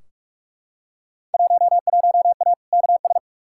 Morse code, heard as a keyed tone, about 42 wpm, 700 Hz. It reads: 11A KS
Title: Field Day Exchanges Encore - 45wpm